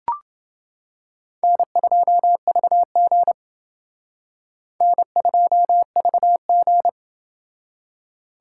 Morse code, heard as a keyed tone, about 29 wpm, 690 Hz. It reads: N2VG N2VG